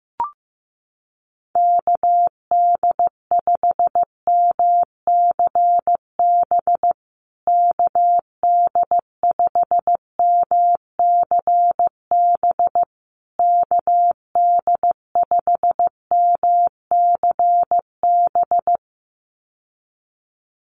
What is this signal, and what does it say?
Morse code, heard as a keyed tone, about 15 wpm, 700 Hz. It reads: KD5MCB KD5MCB KD5MCB
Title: USA Call Signs Repeated 3x 15wpm